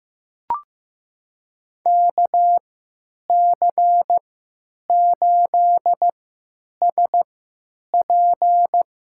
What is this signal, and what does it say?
Morse code, heard as a keyed tone, about 15 wpm, spaced as for 9 wpm, 700 Hz. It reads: KC8SP
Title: USA Call Signs - 3x Character Spacing - 15wpm